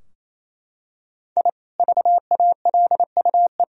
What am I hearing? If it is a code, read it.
Morse code, heard as a keyed tone, about 28 wpm, 700 Hz. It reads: I VALUE
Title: Sentences from Top 1000 Words - Review Part 3 - 28wpm